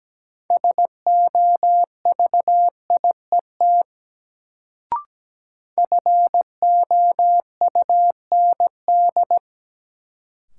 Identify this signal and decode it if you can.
Morse code, heard as a keyed tone, about 17 wpm, 700 Hz. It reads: SOVIET FOUND
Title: Sets of 5 Words Encore - New Word Drill - 17wpm